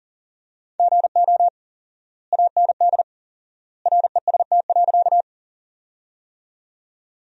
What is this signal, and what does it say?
Morse code, heard as a keyed tone, about 40 wpm, 700 Hz. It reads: GO AND REST.
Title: 4x Word-Spacing - Sentences from Top 500 Words - 40wpm